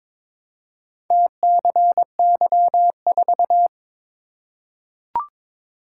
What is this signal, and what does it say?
Morse code, heard as a keyed tone, about 22 wpm, 700 Hz. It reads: TCY4
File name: practice / 22wpm / Four Letter-Number Combinations - 22wpm